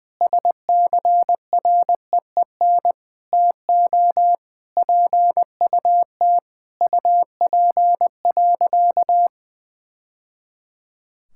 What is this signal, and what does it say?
Morse code, heard as a keyed tone, about 21 wpm, 700 Hz. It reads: SCREEN TO PUT UP.
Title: Sentences from Top 1000 Words - Review Part 6 - 20wpm